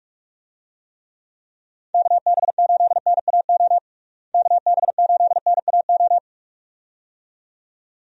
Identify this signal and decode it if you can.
Morse code, heard as a keyed tone, about 44 wpm, 700 Hz. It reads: KB8NAO KB8NAO
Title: USA Call Signs - 6 Characters - Repeated 2x - 45wpm